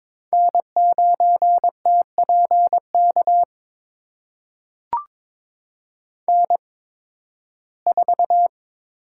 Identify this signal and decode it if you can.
Morse code, heard as a keyed tone, about 21 wpm, spaced as for 14 wpm, 700 Hz. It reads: N9TPK N4
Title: USA Call Signs - 5 Characters - 8x Character Spacing - 22wpm